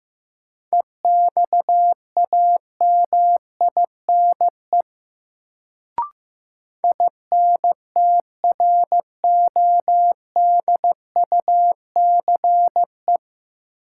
Morse code, heard as a keyed tone, about 15 wpm, 700 Hz. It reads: EXAMINE INTRODUCE
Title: Sentences from Top 1600 Words - New Word Drill Alt - 15wpm